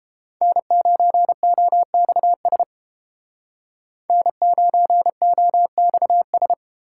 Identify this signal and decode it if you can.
Morse code, heard as a keyed tone, about 30 wpm, 700 Hz. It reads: N9OXS N9OXS